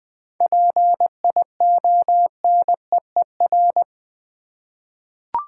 Morse code, heard as a keyed tone, about 21 wpm, 700 Hz. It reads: PIONEER
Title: 7 Letter Words - First 3 Letters Given - 20wpm